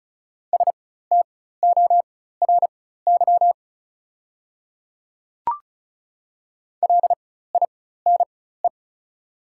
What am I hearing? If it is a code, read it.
Morse code, heard as a keyed tone, about 34 wpm, spaced as for 16 wpm, 690 Hz. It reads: STORY LINE
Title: 4x Character Spacing - Top 500 Words - 35wpm